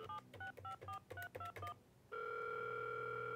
DTMF, heard as telephone touch tones, tones of about 120 ms. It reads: 7654321